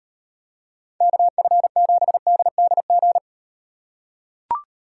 Morse code, heard as a keyed tone, about 38 wpm, 700 Hz. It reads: KF7DDG